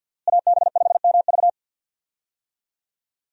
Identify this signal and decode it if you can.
Morse code, heard as a keyed tone, about 52 wpm, 700 Hz. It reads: AB5MV
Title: USA Call Signs - 5 Characters - Repeated 3x - 50wpm